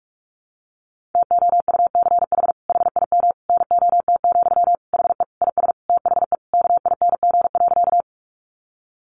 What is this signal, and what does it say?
Morse code, heard as a keyed tone, about 44 wpm, 700 Hz. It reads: TOUCH HIM NOT, HE IS THE KING.
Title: Sentences from Top 1200 Words 45wpm